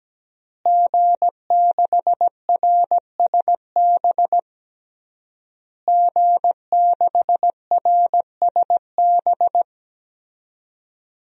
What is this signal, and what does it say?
Morse code, heard as a keyed tone, about 17 wpm, 700 Hz. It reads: G6RSB G6RSB
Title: DXCC-100 Call Signs - 5 Characters - Repeated 3x - 17wpm